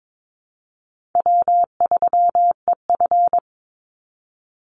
Morse code, heard as a keyed tone, about 22 wpm, 700 Hz. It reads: W3EF
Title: Easy Call Signs - 22wpm